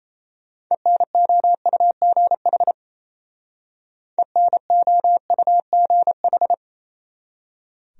Morse code, heard as a keyed tone, about 36 wpm, 700 Hz. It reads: ENOUGH ENOUGH